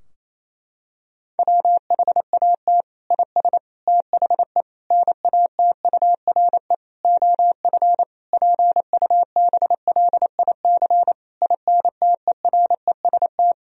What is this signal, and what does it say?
Morse code, heard as a keyed tone, about 28 wpm, 700 Hz. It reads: WHAT IS THE NATURE OF PUBLIC INTEREST
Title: Sentences from Top 1000 Words - Review Part 3 - 28wpm